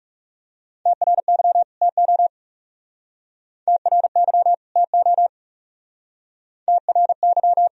Morse code, heard as a keyed tone, about 39 wpm, 700 Hz. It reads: TRY TO TRY TO TRY